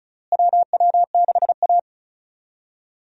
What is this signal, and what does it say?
Morse code, heard as a keyed tone, about 36 wpm, 700 Hz. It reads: WW6A
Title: USA Call Signs - 4 Characters - Repeated 2x - 35wpm